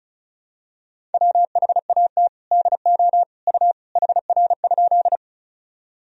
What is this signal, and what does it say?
Morse code, heard as a keyed tone, about 34 wpm, 700 Hz. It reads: WHAT DO U HR?
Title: Sentences from Top 300 Words and Abbreviations 35wpm